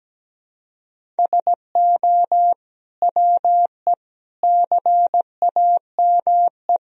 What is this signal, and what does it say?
Morse code, heard as a keyed tone, about 17 wpm, 700 Hz. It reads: SO WE CAME
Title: Sentences from Top 400 Words and Abbreviations 17wpm